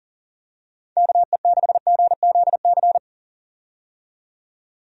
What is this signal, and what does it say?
Morse code, heard as a keyed tone, about 40 wpm, 700 Hz. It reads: KE6GZC
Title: USA Call Signs - 6 Characters - 40wpm